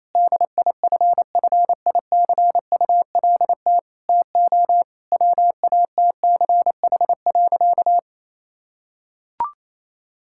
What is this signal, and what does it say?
Morse code, heard as a keyed tone, about 28 wpm, 700 Hz. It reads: DIFFICULT TO WATCH.
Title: Sentences from Top 1000 Words - Review Part 5 - 28wpm